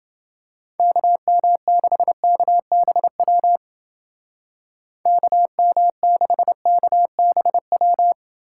Morse code, heard as a keyed tone, about 28 wpm, 700 Hz. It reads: KM6KBW KM6KBW